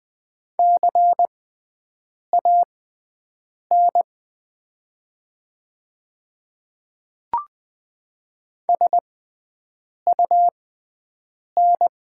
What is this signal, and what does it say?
Morse code, heard as a keyed tone, about 20 wpm, spaced as for 7 wpm, 700 Hz. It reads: CAN SUN
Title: 6x Character Spacing - Top 500 Words - 20wpm